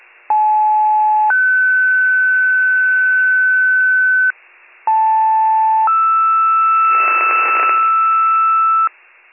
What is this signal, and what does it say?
Two-tone page, heard as a one-second tone, then a three-second tone, about 860 then 1500 Hz, about 880 then 1320 Hz, with noise. It is heard from 0.3 to 8.9 s.